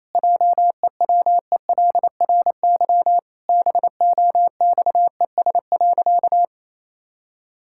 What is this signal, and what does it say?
Morse code, heard as a keyed tone, about 28 wpm, 700 Hz. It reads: JEWELRY BOXES.
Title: Pangram Challenge v2 - 28wpm